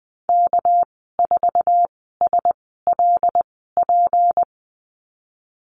Morse code, heard as a keyed tone, about 20 wpm, spaced as for 15 wpm, 700 Hz. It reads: K4SLP
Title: 200 Lesson 32g - Letter k - Call Signs 20wpm